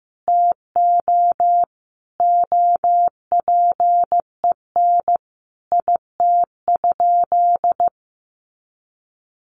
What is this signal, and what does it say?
Morse code, heard as a keyed tone, about 15 wpm, 700 Hz. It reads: TO OPEN IT?
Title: Sentences from Top 600 Words 15wpm